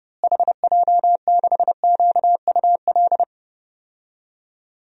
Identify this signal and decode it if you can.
Morse code, heard as a keyed tone, about 30 wpm, 700 Hz. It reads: HJ6QUL
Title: DXCC-100 Call Signs - 6 Characters - Repeated 3x - 30wpm